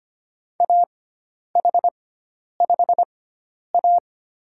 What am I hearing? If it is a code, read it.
Morse code, heard as a keyed tone, about 25 wpm, spaced as for 10 wpm, 700 Hz. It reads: AH5A
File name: Lesson 16g - Letter c - Call Signs 25wpm